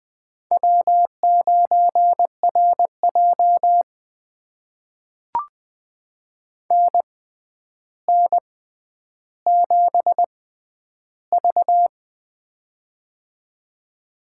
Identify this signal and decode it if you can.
Morse code, heard as a keyed tone, about 21 wpm, spaced as for 10 wpm, 700 Hz. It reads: W9RJ NN7V
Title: USA Call Signs - 4 Characters - 6x Character Spacing - 20wpm